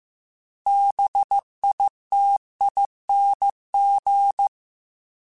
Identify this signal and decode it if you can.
Morse code, heard as a keyed tone, about 15 wpm, 800 Hz. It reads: BITING
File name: Special INGs-mp3s / Special INGs-mp3s / Sngl ing @13